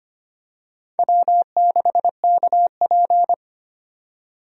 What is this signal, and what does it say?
Morse code, heard as a keyed tone, about 25 wpm, 700 Hz. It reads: W6KP
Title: USA Call Signs - 4 Characters - 25wpm